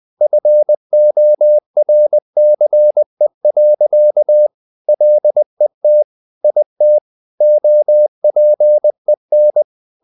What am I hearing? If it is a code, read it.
Morse code, heard as a keyed tone, about 20 wpm, 600 Hz. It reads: FORCE. LET IT OPEN
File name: the old man and the sea / the old man and the sea part 1 20wpm